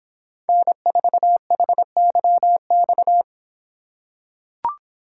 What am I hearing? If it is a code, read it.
Morse code, heard as a keyed tone, about 26 wpm, 700 Hz. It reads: N4HYX